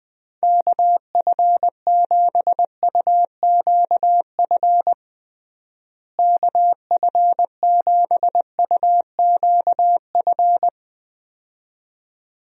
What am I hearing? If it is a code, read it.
Morse code, heard as a keyed tone, about 21 wpm, 700 Hz. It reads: KF7UQF KF7UQF
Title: USA Call Signs - 6 Characters - Repeated 2x - 20wpm